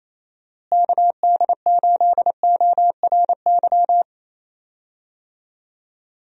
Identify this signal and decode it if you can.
Morse code, heard as a keyed tone, about 28 wpm, 700 Hz. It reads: KD8ORY
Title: USA Call Signs - 6 Characters - Repeated 2x - 28wpm